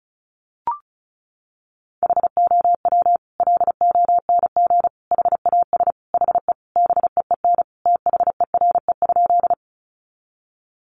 Morse code, heard as a keyed tone, about 34 wpm, 700 Hz. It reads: HOW LONG HAS HE BEEN THERE?
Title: Sentences from Top 100 Words 35wpm